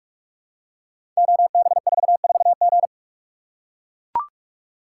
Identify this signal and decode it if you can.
Morse code, heard as a keyed tone, about 44 wpm, 700 Hz. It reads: OB44G
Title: Code Groups - 5 Letter:Numbers - 45wpm